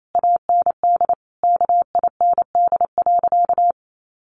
Morse code, heard as a keyed tone, about 28 wpm, 700 Hz. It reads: AND KIND.
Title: Sentences from Top 1500 Words 28wpm